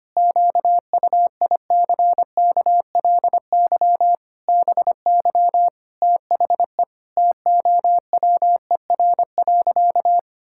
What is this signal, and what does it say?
Morse code, heard as a keyed tone, about 25 wpm, 700 Hz. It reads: QUICKLY BY THE TOWER.
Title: Pangram Challenge v2 - 25wpm